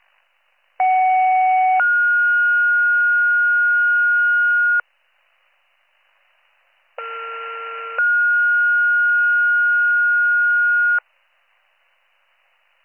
Two-tone page, heard as a one-second tone, then a three-second tone, about 750 then 1430 Hz, about 1370 then 1430 Hz. It is heard from 0.8 to 11.0 s.